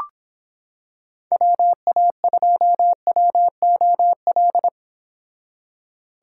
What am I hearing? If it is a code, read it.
Morse code, heard as a keyed tone, about 26 wpm, 700 Hz. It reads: WA2WOL